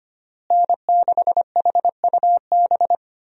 Morse code, heard as a keyed tone, about 25 wpm, 700 Hz. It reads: N6HUB
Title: USA Call Signs - 5 Characters - Repeated 3x - 25wpm